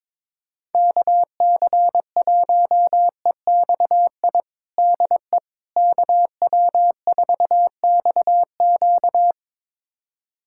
Morse code, heard as a keyed tone, about 22 wpm, 700 Hz. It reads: KC1EXI DE KW4XQ
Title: USA Callsign DE Callsign 22wpm